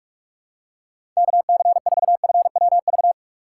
Morse code, heard as a keyed tone, about 44 wpm, 700 Hz. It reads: KC4FWV